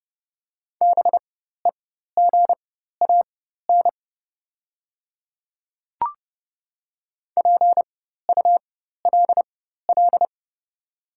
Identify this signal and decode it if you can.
Morse code, heard as a keyed tone, about 29 wpm, spaced as for 14 wpm, 690 Hz. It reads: BEGAN PULL
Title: 4x Character Spacing - Top 500 Words - 30wpm